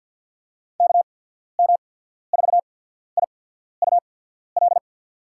Morse code, heard as a keyed tone, about 53 wpm, spaced as for 14 wpm, 700 Hz. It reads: KM4IUL
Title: USA Call Signs - 6 Characters - 8x Character Spacing - 50wpm